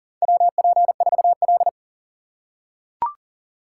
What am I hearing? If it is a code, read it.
Morse code, heard as a keyed tone, about 40 wpm, 700 Hz. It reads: WP4L
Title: USA Call Signs - 4 Characters - 2x Character Spacing - 40wpm